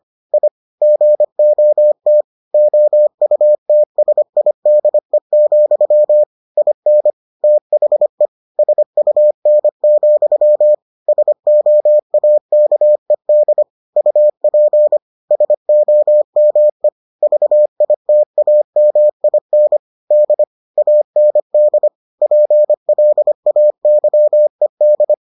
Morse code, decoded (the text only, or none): I GOT OUTSIDE, IN THE SUN, SOAKED UP SOME VITAMIN D AND PLAYED